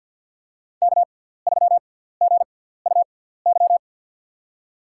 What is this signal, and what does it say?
Morse code, heard as a keyed tone, about 53 wpm, spaced as for 17 wpm, 700 Hz. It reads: K3GUY
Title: USA Call Signs - 6x Character Spacing - 50wpm